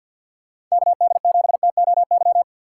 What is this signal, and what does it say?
Morse code, heard as a keyed tone, about 47 wpm, 700 Hz. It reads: KD7TOY
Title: USA Call Signs - 6 Characters - 6x Character Spacing - 50wpm